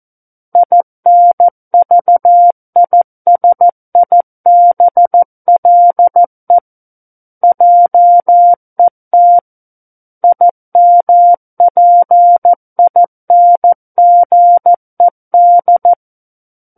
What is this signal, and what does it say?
Morse code, heard as a keyed tone, about 14 wpm, 700 Hz. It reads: INVISIBLE JET IMPINGED
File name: Book One / 05-Chapter 5 - 14 WPM